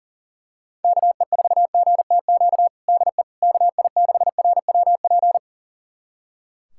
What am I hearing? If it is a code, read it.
Morse code, heard as a keyed tone, about 40 wpm, 700 Hz. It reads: KE4GTQ DE KI6RWP
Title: USA Callsign DE Callsign - 6 Characters - 40wpm